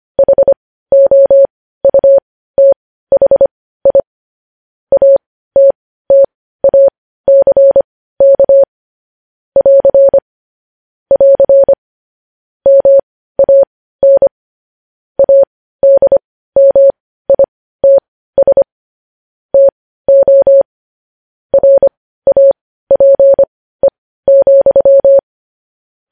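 Morse code, heard as a keyed tone, about 25 wpm, spaced as for 15 wpm, 550 Hz. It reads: HOUTHI ATTACK <AR> <AR> MAN ADMITS TO RAPE,
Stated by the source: Morse code news headlines sent at 15 words per minute.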